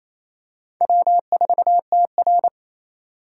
Morse code, heard as a keyed tone, about 27 wpm, 700 Hz. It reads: W4TR
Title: USA Call Signs - 4 Characters - 4x Character Spacing - 28wpm